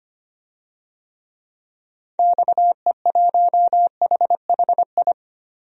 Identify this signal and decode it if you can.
Morse code, heard as a keyed tone, about 25 wpm, 700 Hz. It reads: XE1HHI
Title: DXCC-100 Call Signs - 6 Characters - Repeated 3x - 25wpm